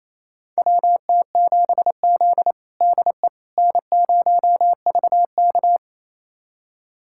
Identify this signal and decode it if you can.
Morse code, heard as a keyed tone, about 28 wpm, 700 Hz. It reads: WT7Z DE N0VK
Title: USA Callsign DE Callsign - 4 Characters - 28wpm